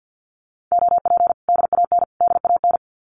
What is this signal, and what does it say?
Morse code, heard as a keyed tone, about 50 wpm, 700 Hz. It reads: OP DAN DAN
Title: Common QSO Phrases Encore 50wpm